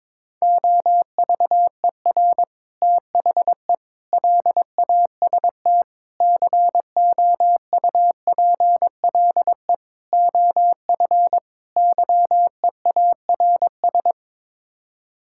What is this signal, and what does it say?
Morse code, heard as a keyed tone, about 22 wpm, 700 Hz. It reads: OVER THE LAST COUPLE OF YEARS
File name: practice / 22wpm / Sets of 6 Words - 22wpm